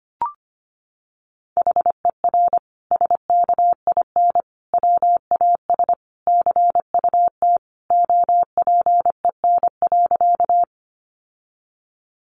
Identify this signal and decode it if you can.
Morse code, heard as a keyed tone, about 25 wpm, 700 Hz. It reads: HER SKIN WAS CUT OPEN.
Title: Sentences from Top 1300 Words 25wpm